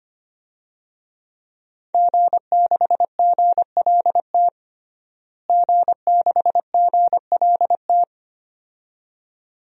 Morse code, heard as a keyed tone, about 25 wpm, 700 Hz. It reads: G6GLT G6GLT
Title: DXCC-100 Call Signs - 5 Characters - Repeated 2x - 25wpm